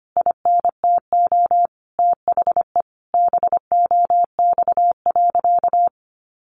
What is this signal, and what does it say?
Morse code, heard as a keyed tone, about 25 wpm, 700 Hz. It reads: INTO THE BOX.
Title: Sentences from Top 1000 Words 25wpm